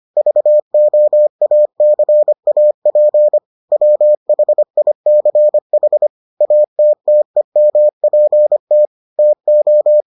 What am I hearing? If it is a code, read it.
Morse code, heard as a keyed tone, about 25 wpm, 600 Hz. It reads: VOACAP WHICH ATTEMPT TO